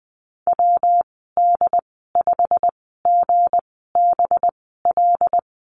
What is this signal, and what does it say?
Morse code, heard as a keyed tone, about 20 wpm, spaced as for 15 wpm, 700 Hz. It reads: WD5GBL
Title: 193 Lesson 31g - Letter v - Call Signs 20wpm